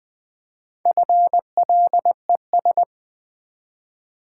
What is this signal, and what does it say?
Morse code, heard as a keyed tone, about 20 wpm, 700 Hz. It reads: FLES
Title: Build A Word - 1 Letter At A Time - 5-Letter Words - 20wpm